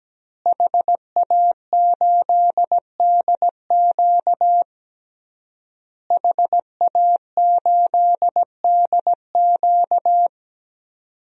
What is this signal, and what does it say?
Morse code, heard as a keyed tone, about 17 wpm, 700 Hz. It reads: HA8DQ HA8DQ
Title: DXCC-100 Call Signs - 5 Characters - Repeated 3x - 17wpm